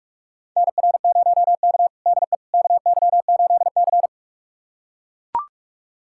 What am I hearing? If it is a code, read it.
Morse code, heard as a keyed tone, about 43 wpm, 700 Hz. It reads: NR0K DE KY8C
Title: USA Callsign DE Callsign - 4 Characters - 45wpm